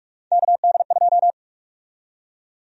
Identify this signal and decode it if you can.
Morse code, heard as a keyed tone, about 45 wpm, 700 Hz. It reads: KD2
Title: USA Call Sign Prefixes - 45wpm